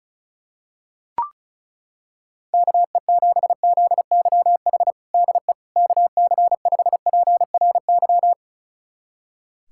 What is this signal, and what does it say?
Morse code, heard as a keyed tone, about 34 wpm, 700 Hz. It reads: KE7ZYH DE KC5PRY
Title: USA Callsign DE Callsign - 6 Characters - 35wpm